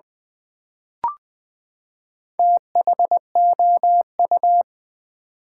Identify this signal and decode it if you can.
Morse code, heard as a keyed tone, about 20 wpm, 700 Hz. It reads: THOU
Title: Build A Word - 4 Letters At A Time - 10-Letter Words - 20wpm